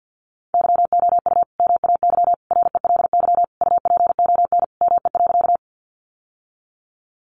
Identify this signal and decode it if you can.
Morse code, heard as a keyed tone, about 50 wpm, 700 Hz. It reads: YOU MAY RELY UPON ME.
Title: Sentences from Top 1200 Words 50wpm